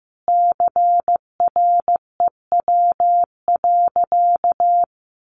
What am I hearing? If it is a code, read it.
Morse code, heard as a keyed tone, about 15 wpm, 700 Hz. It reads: CREW.
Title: Sentences from Top 1500 Words 15wpm